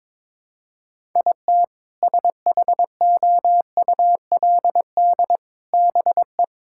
Morse code, heard as a keyed tone, about 21 wpm, 700 Hz. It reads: IT SHOULD BE
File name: practice / 22wpm / Sentences with 2-Word Phrases - 22wpm